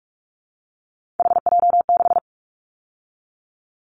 Morse code, heard as a keyed tone, about 46 wpm, 700 Hz. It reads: HJ6
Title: Callsign Prefixes for Top 100 DXCC - 45wpm